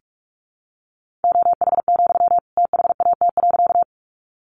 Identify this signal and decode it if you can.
Morse code, heard as a keyed tone, about 44 wpm, 700 Hz. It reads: OH, THAT.
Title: Sentences from Top 500 Words 45wpm